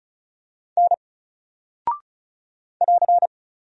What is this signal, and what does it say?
Morse code, heard as a keyed tone, about 36 wpm, 700 Hz. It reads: N <AR>
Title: Single Character (Letter:Number:Punctuation:Prosign) Alt 35wpm